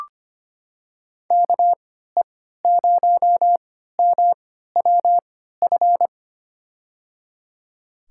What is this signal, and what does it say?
Morse code, heard as a keyed tone, about 25 wpm, spaced as for 14 wpm, 700 Hz. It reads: KE0MWF
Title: USA Call Signs - 3x Character Spacing - 25wpm